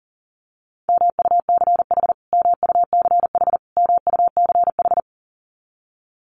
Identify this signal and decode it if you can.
Morse code, heard as a keyed tone, about 40 wpm, 700 Hz. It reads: MUCH MUCH MUCH
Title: Top 300 Words Repeated 40wpm